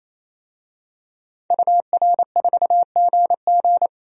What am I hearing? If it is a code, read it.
Morse code, heard as a keyed tone, about 28 wpm, 700 Hz. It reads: UR4GG
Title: DXCC-100 Call Signs - 5 Characters - 28wpm